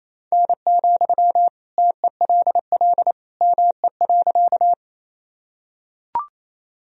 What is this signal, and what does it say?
Morse code, heard as a keyed tone, about 28 wpm, 700 Hz. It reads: N, TELL ME.
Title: Sentences from Top 200 Words and Abbreviations 28wpm